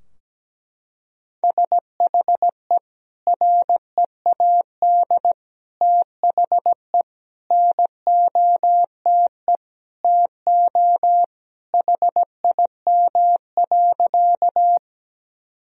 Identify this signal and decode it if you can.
Morse code, heard as a keyed tone, about 17 wpm, 700 Hz. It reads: SHE READ THE NOTE TO HIM.
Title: Sentences from Top 1000 Words - Review Part 2 - 17wpm